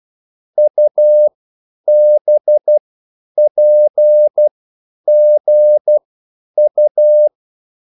Morse code, heard as a keyed tone, about 12 wpm, spaced as for 9 wpm, 600 Hz. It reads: UBPGU